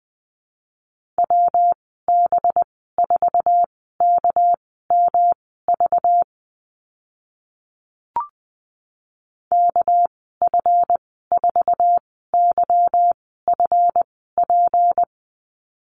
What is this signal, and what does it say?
Morse code, heard as a keyed tone, about 20 wpm, spaced as for 14 wpm, 700 Hz. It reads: WB4KMV KF4YFP
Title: 213 Lesson 34g - Number 8 - Call Signs 20wpm